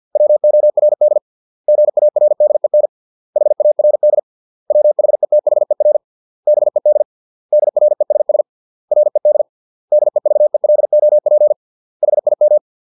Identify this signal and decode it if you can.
Morse code, heard as a keyed tone, about 50 wpm, 600 Hz. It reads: WORD GARDEN HARD WHETHER BED DRESS RED DEVELOP HIM